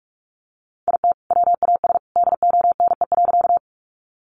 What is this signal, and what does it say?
Morse code, heard as a keyed tone, about 44 wpm, 700 Hz. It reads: IT WAS DONE.
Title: Sentences from Top 500 Words 45wpm